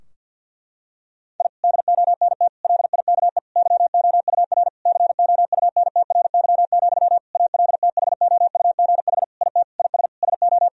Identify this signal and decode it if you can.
Morse code, heard as a keyed tone, about 48 wpm, 700 Hz. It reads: I DONT LIKE YOUR COUNTRY, ALTHOUGH IT IS SO
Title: Sentences from Top 1000 Words - Review Part 3 - 50wpm